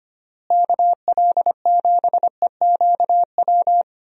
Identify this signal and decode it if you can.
Morse code, heard as a keyed tone, about 25 wpm, 700 Hz. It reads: KL7EQW
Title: DXCC-100 Call Signs - 6 Characters - Repeated 3x - 25wpm